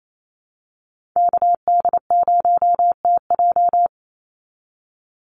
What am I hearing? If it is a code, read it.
Morse code, heard as a keyed tone, about 28 wpm, 700 Hz. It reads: KD0TJ
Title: USA Call Signs 28wpm